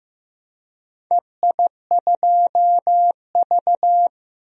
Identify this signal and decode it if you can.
Morse code, heard as a keyed tone, about 16 wpm, 700 Hz. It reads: EI2V
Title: DXCC-100 Call Signs - 4 Characters - Repeated 2x - 15wpm